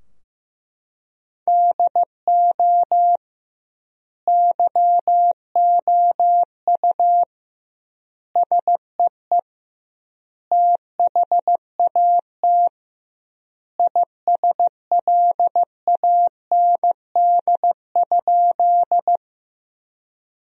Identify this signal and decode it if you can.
Morse code, heard as a keyed tone, about 15 wpm, 700 Hz. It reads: DO YOU SEE THAT ISLAND?
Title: 2x Word-Spacing - Sentences from Top 500 Words - 15wpm